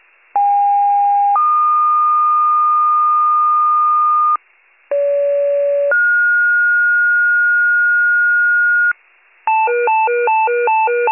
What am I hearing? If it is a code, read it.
Two-tone page, heard as a one-second tone, then a three-second tone, about 810 then 1220 Hz, about 570 then 1430 Hz, with noise. It is heard from 0.3 to 8.9 s.